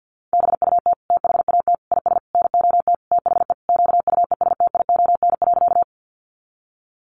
Morse code, heard as a keyed tone, about 48 wpm, 700 Hz. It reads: BUT THAT IS NOT THE QUESTION.
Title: Sentences from Top 600 Words 50wpm